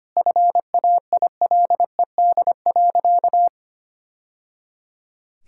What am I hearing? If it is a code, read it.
Morse code, heard as a keyed tone, about 25 wpm, 700 Hz. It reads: FAILED.
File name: practice / 25wpm / Sentences with 2-Word Phrases - 25wpm